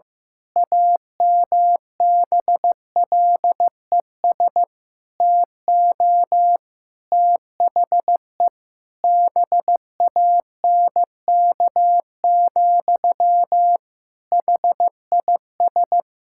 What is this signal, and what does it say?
Morse code, heard as a keyed tone, about 16 wpm, 700 Hz. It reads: AMBLES TO THE BANK, HIS
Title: Pangram Challenge v2 - 15wpm